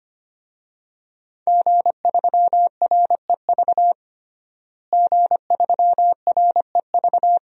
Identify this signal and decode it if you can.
Morse code, heard as a keyed tone, about 25 wpm, 700 Hz. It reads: G3REV G3REV
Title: DXCC-100 Call Signs - 5 Characters - Repeated 3x - 25wpm